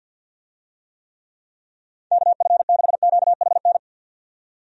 Morse code, heard as a keyed tone, about 47 wpm, 700 Hz. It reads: KF6QHN
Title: USA Call Signs - 6 Characters - 50wpm